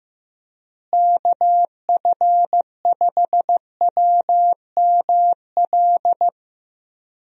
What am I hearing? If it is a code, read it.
Morse code, heard as a keyed tone, about 15 wpm, 700 Hz. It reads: KF5WML
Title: USA Call Signs - 6 Characters - Repeated 3x - 15wpm